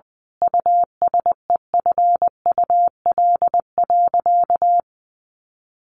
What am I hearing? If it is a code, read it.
Morse code, heard as a keyed tone, about 20 wpm, 700 Hz. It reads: USEFUL.
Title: Sentences from Top 1400 Words 20wpm